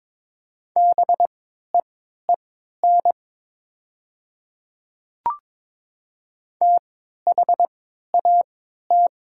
Morse code, heard as a keyed tone, about 22 wpm, spaced as for 12 wpm, 700 Hz. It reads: BEEN THAT
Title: 3x Character Spacing - Top 100 Words - 22wpm